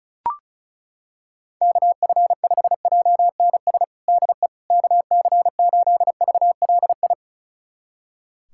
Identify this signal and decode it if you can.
Morse code, heard as a keyed tone, about 36 wpm, 700 Hz. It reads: KF5JNS DE KC8VLI
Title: USA Callsign DE Callsign - 6 Characters - 35wpm